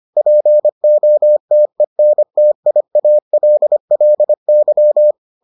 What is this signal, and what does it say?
Morse code, heard as a keyed tone, about 25 wpm, 600 Hz. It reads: POTENTIALLY